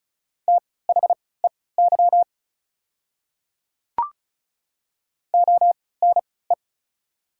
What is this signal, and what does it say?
Morse code, heard as a keyed tone, about 34 wpm, spaced as for 20 wpm, 690 Hz. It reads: THEY ONE
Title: 3x Character Spacing - Top 100 Words - 35wpm